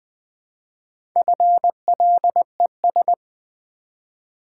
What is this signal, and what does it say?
Morse code, heard as a keyed tone, about 20 wpm, 700 Hz. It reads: FLES
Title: Build A Word - 1 Letter At A Time - 5-Letter Words - 20wpm